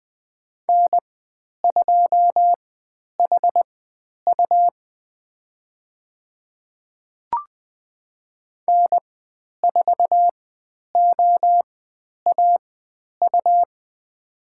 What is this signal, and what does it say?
Morse code, heard as a keyed tone, about 20 wpm, spaced as for 10 wpm, 700 Hz. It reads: N2HU N4OAU
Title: 102 Lesson 17g - Letter u - Call Signs 20wpm